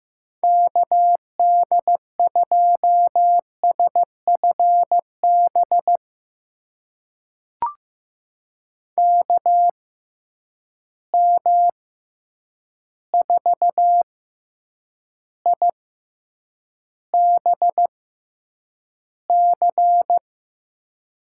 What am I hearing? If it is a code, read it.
Morse code, heard as a keyed tone, about 15 wpm, spaced as for 8 wpm, 700 Hz. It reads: KD2SFB KM4IBC
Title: USA Call Signs - 6 Characters - 6x Character Spacing - 15wpm